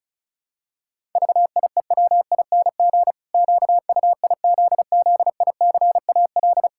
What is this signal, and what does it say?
Morse code, heard as a keyed tone, about 36 wpm, 700 Hz. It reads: VIEWING QUIZZICAL
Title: Pangram Challenge v2 - 35wpm